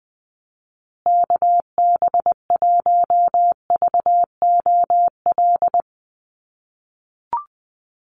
Morse code, heard as a keyed tone, about 20 wpm, 700 Hz. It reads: KB1VOL
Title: USA Call Signs Repeated 3x 20wpm